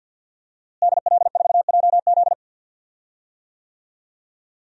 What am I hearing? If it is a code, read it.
Morse code, heard as a keyed tone, about 52 wpm, 700 Hz. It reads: DL4JZ
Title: DXCC-100 Call Signs - 5 Characters - Repeated 3x - 50wpm